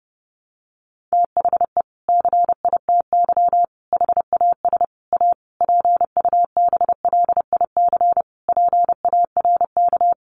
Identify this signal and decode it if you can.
Morse code, heard as a keyed tone, about 30 wpm, 700 Hz. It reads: THE CITY HAS A PUBLIC PARK
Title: Sentences from Top 1300 Words 30wpm